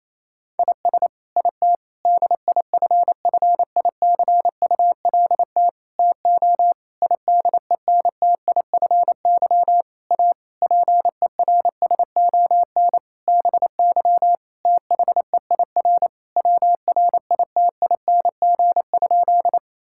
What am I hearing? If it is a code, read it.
Morse code, heard as a keyed tone, about 27 wpm, 700 Hz. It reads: IS IT DIFFICULT TO IDENTIFY A PERSON BY THEIR WRITING?
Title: Sentences from Top 1000 Words - Review Part 6 - 28wpm